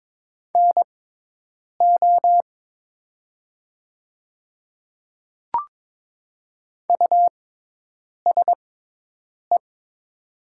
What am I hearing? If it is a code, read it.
Morse code, heard as a keyed tone, about 22 wpm, spaced as for 8 wpm, 700 Hz. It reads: NO USE
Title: 6x Character Spacing - Top 100 Words - 22wpm